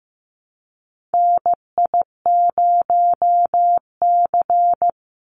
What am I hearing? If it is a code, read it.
Morse code, heard as a keyed tone, about 15 wpm, 700 Hz. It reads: NI0C
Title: Call Signs Repeated 3x - 15wpm